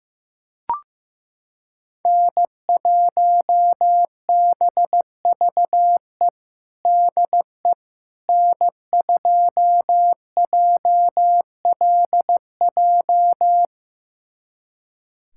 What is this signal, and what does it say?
Morse code, heard as a keyed tone, about 15 wpm, 700 Hz. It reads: N1BVE DE N2JLJ
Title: USA Callsign DE Callsign - 5 Characters - 15wpm